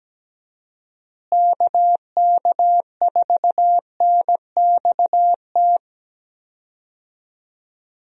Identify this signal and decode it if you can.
Morse code, heard as a keyed tone, about 17 wpm, 700 Hz. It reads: KK4NXT
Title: USA Call Signs - 6 Characters - Repeated 2x - 17wpm